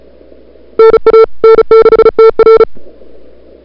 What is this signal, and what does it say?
Morse code, heard as a keyed tone, about 36 wpm, 440 Hz. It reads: NA N6TR
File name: CWSprint / February 2007 CW Sprint / 01-2007 CW Sprint Hour1